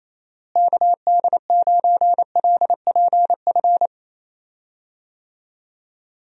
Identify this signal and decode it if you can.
Morse code, heard as a keyed tone, about 27 wpm, 700 Hz. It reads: KD9LPF
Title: USA Call Signs - 6 Characters - Repeated 2x - 28wpm